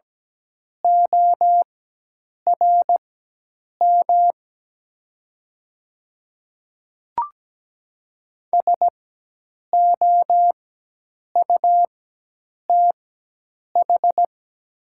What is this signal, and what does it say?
Morse code, heard as a keyed tone, about 17 wpm, spaced as for 8 wpm, 700 Hz. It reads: ORM SOUTH
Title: 4x Character Spacing - Top 500 Words - 17wpm